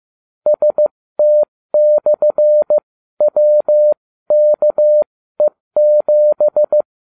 Morse code, heard as a keyed tone, about 15 wpm, spaced as for 13 wpm, 600 Hz. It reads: ST/WKE7